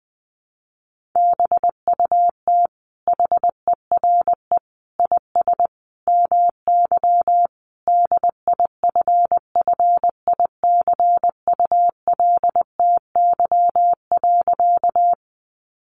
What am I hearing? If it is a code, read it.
Morse code, heard as a keyed tone, about 20 wpm, 700 Hz. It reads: BUT HERE IS MY DIFFICULTY.
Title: Sentences from Top 1500 Words 20wpm